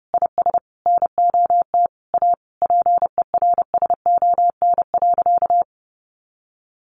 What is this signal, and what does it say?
Morse code, heard as a keyed tone, about 30 wpm, 700 Hz. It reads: IS NOT A PERSON.
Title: Sentences from Top 1500 Words 30wpm